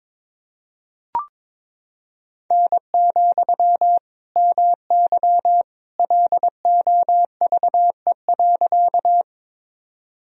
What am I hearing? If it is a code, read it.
Morse code, heard as a keyed tone, about 22 wpm, 700 Hz. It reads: N, MY LOVE.
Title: Sentences from Top 400 Words and Abbreviations 22wpm